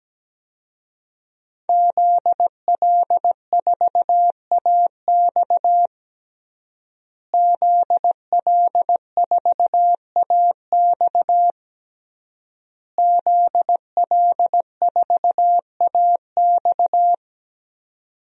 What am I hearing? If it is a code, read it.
Morse code, heard as a keyed tone, about 17 wpm, 700 Hz. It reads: ZL4AX ZL4AX ZL4AX
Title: DXCC-100 Call Signs - 5 Characters - Repeated 3x - 17wpm